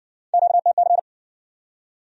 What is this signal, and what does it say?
Morse code, heard as a keyed tone, about 55 wpm, 700 Hz. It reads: QTC